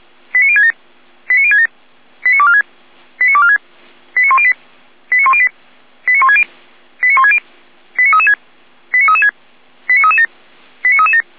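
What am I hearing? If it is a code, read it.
Five-tone selective call (ZVEI), heard as quick runs of five tones, tones of about 70 ms. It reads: 89067 89067 89246 89246 89108 89108 89160 89160 89336 89336 89337 89337
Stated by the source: Five-tone ZVEI selective call received by SDR.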